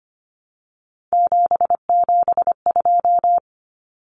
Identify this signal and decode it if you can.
Morse code, heard as a keyed tone, about 25 wpm, 700 Hz. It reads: 772
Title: Numbers - 3-Digit 25wpm